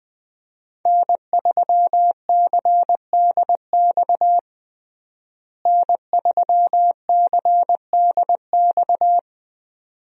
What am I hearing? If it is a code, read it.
Morse code, heard as a keyed tone, about 20 wpm, 700 Hz. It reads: N3CDX N3CDX
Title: USA Call Signs - 5 Characters - Repeated 2x - 20wpm